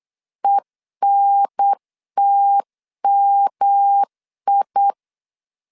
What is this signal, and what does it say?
Morse code, heard as a keyed tone, about 8 wpm, 800 Hz. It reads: ENTMI